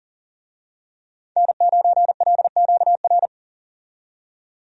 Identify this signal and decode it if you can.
Morse code, heard as a keyed tone, about 40 wpm, 700 Hz. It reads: N9LQR
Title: USA Call Signs - 5 Characters - 40wpm